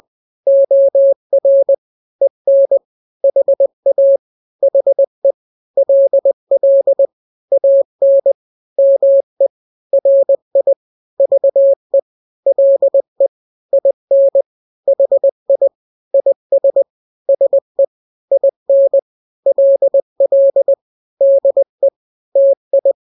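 Morse code, heard as a keyed tone, about 20 wpm, 550 Hz. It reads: OR EN HA HE LL AN ME RI VE LE IN HI IS SE IN LL DE TI
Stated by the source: Morse drill of two-letter n-grams at 20/18 WPM (Farnsworth), from KK7RR.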